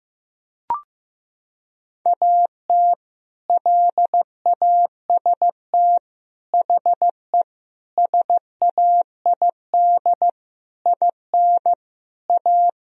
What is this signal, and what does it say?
Morse code, heard as a keyed tone, about 15 wpm, 700 Hz. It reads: AT LAST HE SAID IN A
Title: Sentences from Top 1000 Words - Review Part 1 - 15wpm